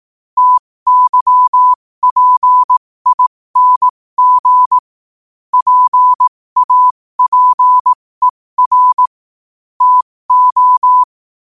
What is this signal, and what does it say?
Morse code, heard as a keyed tone, about 18 wpm, spaced as for 15 wpm, 1000 Hz. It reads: TYPING PAPER TO
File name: qotdfiles / 15wpm